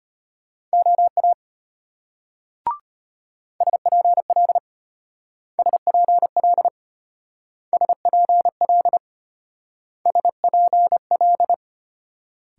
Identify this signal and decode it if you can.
Morse code, heard as a keyed tone, about 34 wpm, 700 Hz. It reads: OA SPL SPL SPL SPL